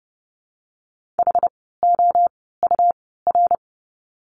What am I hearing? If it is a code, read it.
Morse code, heard as a keyed tone, about 30 wpm, spaced as for 17 wpm, 700 Hz. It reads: HOUR
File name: course / 30:10wpm / 108 Lesson 18f - Letter m - Words 30wpm